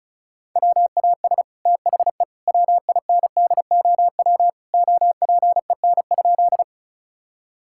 Morse code, heard as a keyed tone, about 36 wpm, 700 Hz. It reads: WAS THE WINDOW OPEN?
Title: Sentences from Top 1600 Words 35wpm